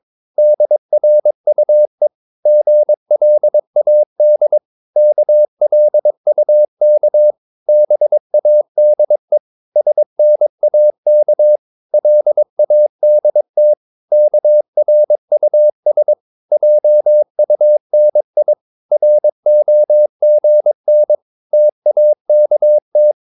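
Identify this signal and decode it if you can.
Morse code, heard as a keyed tone, about 22 wpm, 600 Hz. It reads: DRUE GLAD KLUK BADE SNAK LADT KRUS JUNI ROGN TAKT
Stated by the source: Morse practice of short Danish words at 22 wpm, from CW med Gnister.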